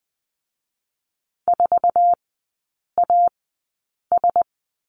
Morse code, heard as a keyed tone, about 19 wpm, spaced as for 8 wpm, 690 Hz. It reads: 4AS
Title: 060 Lesson 11e - Letter h - Three Characters 20wpm